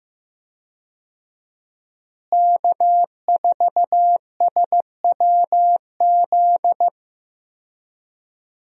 Morse code, heard as a keyed tone, about 15 wpm, 700 Hz. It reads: K4SWZ
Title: USA Call Signs - 5 Characters - Repeated 2x - 15wpm